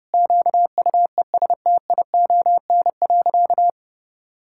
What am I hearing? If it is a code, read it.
Morse code, heard as a keyed tone, about 30 wpm, 700 Hz. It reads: QUESTION.
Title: Sentences from Top 1000 Words - Review Part 5 - 30wpm